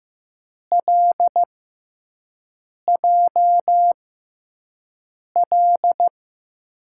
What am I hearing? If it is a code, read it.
Morse code, heard as a keyed tone, about 15 wpm, spaced as for 5 wpm, 700 Hz. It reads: LJL
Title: USA Call Signs - 6 Characters - 6x Character Spacing - 15wpm